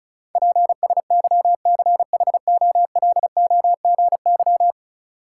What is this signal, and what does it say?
Morse code, heard as a keyed tone, about 36 wpm, 700 Hz. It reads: PSYCHOLOGY
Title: Sentences from Top 1400 Words - New Word Drill Alt - 35wpm